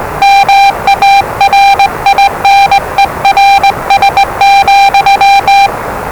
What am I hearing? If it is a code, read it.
Morse code, heard as a keyed tone, about 20 wpm, 790 Hz, with noise. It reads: MARINERS,
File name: WellingtonRadioZlwCloseDownOnHfAndMf / 01-Zlw 4277khz 417.5 qrt until 0925z